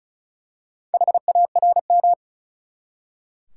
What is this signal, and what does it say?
Morse code, heard as a keyed tone, about 34 wpm, 700 Hz. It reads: HARM
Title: Four Letter Words - 35wpm